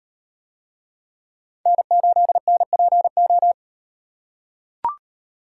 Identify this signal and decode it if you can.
Morse code, heard as a keyed tone, about 37 wpm, 700 Hz. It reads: N8NPO